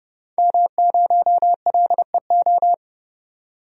Morse code, heard as a keyed tone, about 30 wpm, 700 Hz. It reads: M0LEO
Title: DXCC-100 Call Signs - 5 Characters - 30wpm